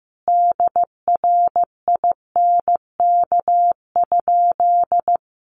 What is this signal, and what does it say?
Morse code, heard as a keyed tone, about 15 wpm, 700 Hz. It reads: DRINK?
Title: Sentences from Top 1500 Words 15wpm